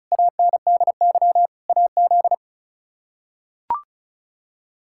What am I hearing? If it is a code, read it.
Morse code, heard as a keyed tone, about 36 wpm, 700 Hz. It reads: ANDY AZ
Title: K1USN SST QSO Practice - 35wpm